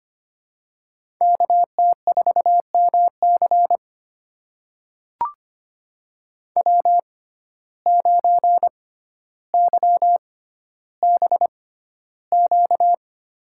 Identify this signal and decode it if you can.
Morse code, heard as a keyed tone, about 25 wpm, spaced as for 13 wpm, 700 Hz. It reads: KT4MC W9YBQ
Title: USA Call Signs - 5 Characters - 6x Character Spacing - 25wpm